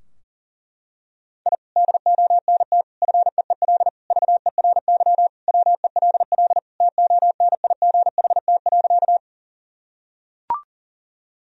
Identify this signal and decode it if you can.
Morse code, heard as a keyed tone, about 40 wpm, 700 Hz. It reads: I DONT FEEL VERY WELL TONIGHT.
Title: Sentences from Top 1000 Words - Review Part 3 - 40wpm